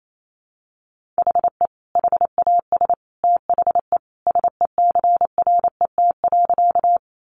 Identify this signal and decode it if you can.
Morse code, heard as a keyed tone, about 28 wpm, 700 Hz. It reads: HE HAS THE SECRET.
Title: Sentences from Top 1200 Words 28wpm